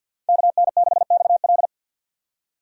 Morse code, heard as a keyed tone, about 52 wpm, 700 Hz. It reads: KN6XL
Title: USA Call Signs - 5 Characters - 2x Character Spacing - 50wpm